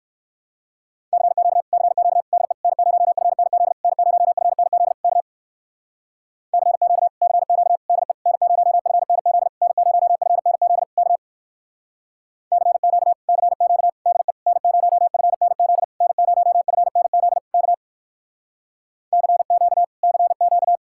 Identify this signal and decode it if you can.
Morse code, heard as a keyed tone, about 58 wpm, 700 Hz. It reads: CQ CQ DE N0FNZ N0FNZ K CQ CQ DE N0FNZ N0FNZ K CQ CQ DE N0FNZ N0FNZ K CQ CQ